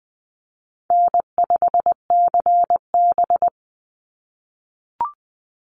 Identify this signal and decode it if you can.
Morse code, heard as a keyed tone, about 20 wpm, 700 Hz. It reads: N5CB
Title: USA Call Signs 20wpm